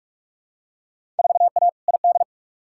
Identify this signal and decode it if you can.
Morse code, heard as a keyed tone, about 43 wpm, 700 Hz. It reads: 4A ID
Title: Field Day Exchanges - 45wpm